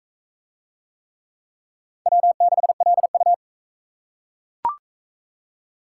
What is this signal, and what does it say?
Morse code, heard as a keyed tone, about 37 wpm, 700 Hz. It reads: W6LU